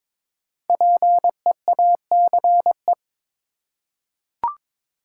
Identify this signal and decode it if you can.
Morse code, heard as a keyed tone, about 22 wpm, 700 Hz. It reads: PEACE
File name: 5 Letter Words - First 3 Letters Given - 22wpm